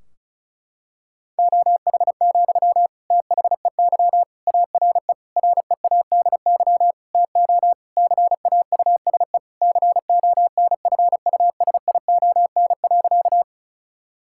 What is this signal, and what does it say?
Morse code, heard as a keyed tone, about 36 wpm, 700 Hz. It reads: OH, THEY ARE READY TO CAUSE CONFUSION.
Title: Sentences from Top 1600 Words 35wpm